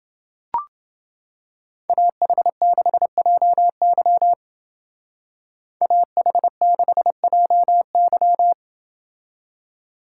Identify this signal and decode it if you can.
Morse code, heard as a keyed tone, about 29 wpm, 700 Hz. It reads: AH6JY AH6JY